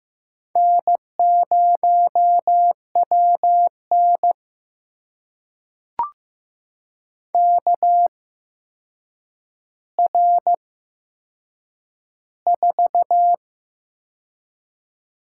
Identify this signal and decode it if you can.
Morse code, heard as a keyed tone, about 16 wpm, spaced as for 7 wpm, 700 Hz. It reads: N0WN KR4
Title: USA Call Signs - 4 Characters - 8x Character Spacing - 15wpm